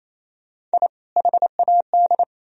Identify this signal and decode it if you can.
Morse code, heard as a keyed tone, about 33 wpm, 700 Hz. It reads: I HAD
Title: Speed-Racing - Sets of 2 Words 28wpm